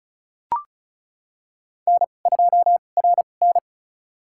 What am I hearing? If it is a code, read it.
Morse code, heard as a keyed tone, about 34 wpm, spaced as for 25 wpm, 700 Hz. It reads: N2RN
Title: USA Call Signs - 4 Characters - 2x Character Spacing - 35wpm